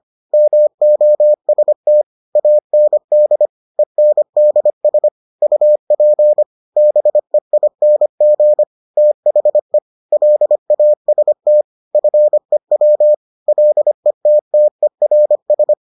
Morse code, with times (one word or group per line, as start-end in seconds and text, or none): 0.3-2.0 MOST
2.4-3.5 AND
3.8-5.1 ENDS
5.4-6.4 UP
6.8-8.6 BEING
9.0-9.8 THE
10.1-11.6 LAST
11.9-13.1 FEW
13.5-15.7 LETTERS